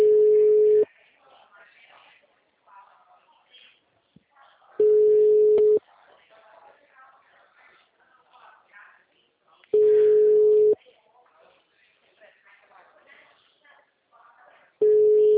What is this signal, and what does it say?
Ringback tone (Europe, 425 Hz), with speech.